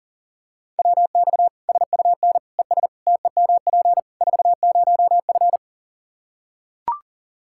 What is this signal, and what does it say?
Morse code, heard as a keyed tone, about 40 wpm, 700 Hz. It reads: WX SUN ES TEMP 40F
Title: Practice QSO - Exchange 2 - Info - 40wpm